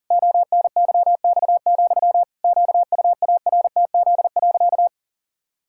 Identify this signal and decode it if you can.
Morse code, heard as a keyed tone, about 40 wpm, 700 Hz. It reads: ONYX, QUARTZ.
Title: Pangram Challenge v2 - 40wpm